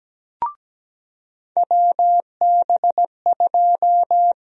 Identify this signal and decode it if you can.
Morse code, heard as a keyed tone, about 17 wpm, 700 Hz. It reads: WB2
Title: USA Call Sign Prefixes - Repeated 2x - 17wpm